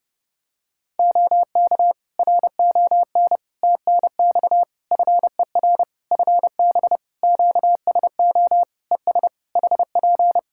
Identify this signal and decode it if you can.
Morse code, heard as a keyed tone, about 32 wpm, 700 Hz. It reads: OK RON TNX FER FB QSO ES HP
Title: Practice QSO - Exchange 4 - Ending - 30wpm